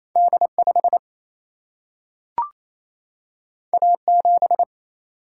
Morse code, heard as a keyed tone, about 28 wpm, 700 Hz. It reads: D5 A7
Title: Single-Letter Single-Number 28wpm